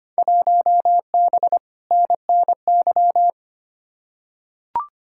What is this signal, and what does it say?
Morse code, heard as a keyed tone, about 25 wpm, 700 Hz. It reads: 1B NNY
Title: Field Day Exchanges - 25wpm